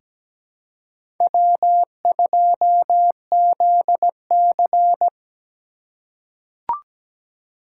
Morse code, heard as a keyed tone, about 17 wpm, 700 Hz. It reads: W2ZC
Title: USA Call Signs - 4 Characters - 4x Character Spacing - 17wpm